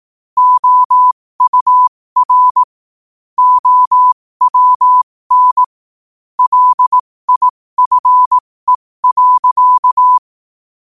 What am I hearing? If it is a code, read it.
Morse code, heard as a keyed tone, about 18 wpm, spaced as for 15 wpm, 1000 Hz. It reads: OUR OWN LIFE.
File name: 2024-11-18-15wpm